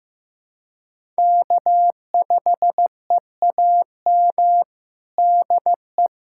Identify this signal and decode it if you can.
Morse code, heard as a keyed tone, about 16 wpm, 700 Hz. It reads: K5EAM DE
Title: USA Callsign DE Callsign - 5 Characters - 15wpm